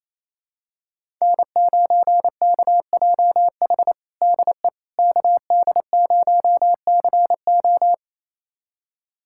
Morse code, heard as a keyed tone, about 28 wpm, 700 Hz. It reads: N9KJH DE KD0CO
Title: USA Callsign DE Callsign - 5 Characters - 28wpm